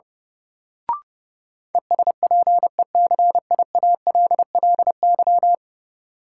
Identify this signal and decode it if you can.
Morse code, heard as a keyed tone, about 30 wpm, 700 Hz. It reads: ESPECIALLY
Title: Sentences from Top 700 Words - New Word Drill - 30wpm